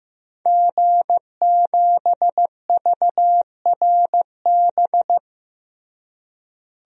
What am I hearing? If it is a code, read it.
Morse code, heard as a keyed tone, about 15 wpm, 700 Hz. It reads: G7VRB
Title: DXCC-100 Call Signs - 5 Characters - Repeated 3x - 15wpm